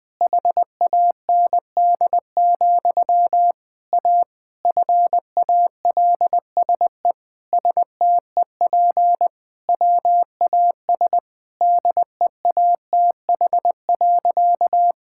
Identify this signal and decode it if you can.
Morse code, heard as a keyed tone, about 20 wpm, 700 Hz. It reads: HAND, A FALSE STEP WAS DEATH.
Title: Sentences from Top 1600 Words 20wpm